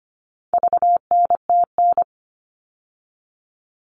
Morse code, heard as a keyed tone, about 25 wpm, 700 Hz. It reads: VNTN
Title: All Cut Numbers - 4-Digit 25wpm